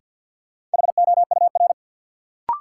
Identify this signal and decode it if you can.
Morse code, heard as a keyed tone, about 51 wpm, 700 Hz. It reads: HOUR
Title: Sets of 5 Words - New Word Drill - 50wpm